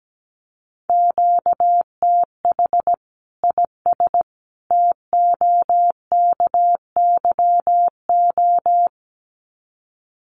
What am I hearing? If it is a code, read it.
Morse code, heard as a keyed tone, about 17 wpm, 700 Hz. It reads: QTH IS TOKYO
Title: Common QSO Phrases 17wpm